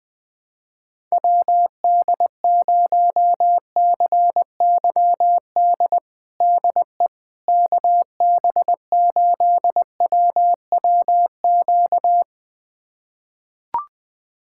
Morse code, heard as a keyed tone, about 21 wpm, 700 Hz. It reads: WD0CYD DE KB8WWQ
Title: USA Callsign DE Callsign - 6 Characters - 20wpm